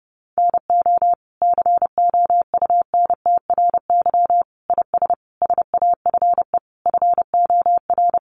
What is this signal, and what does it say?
Morse code, heard as a keyed tone, about 30 wpm, 700 Hz. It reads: NO COUNTRY IS SAFE FOR
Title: Sentences from Top 900 Words 30wpm